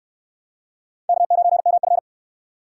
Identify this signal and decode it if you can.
Morse code, heard as a keyed tone, about 69 wpm, 700 Hz. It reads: K9MF